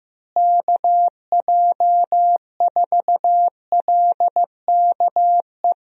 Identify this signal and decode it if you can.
Morse code, heard as a keyed tone, about 15 wpm, 700 Hz. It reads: KJ4LKE
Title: USA Call Signs - 6 Characters - Repeated 2x - 15wpm